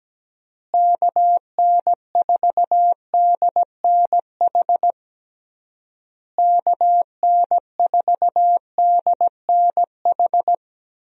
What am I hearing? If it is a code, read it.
Morse code, heard as a keyed tone, about 17 wpm, 700 Hz. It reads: KN4DNH KN4DNH
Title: USA Call Signs - 6 Characters - Repeated 3x - 17wpm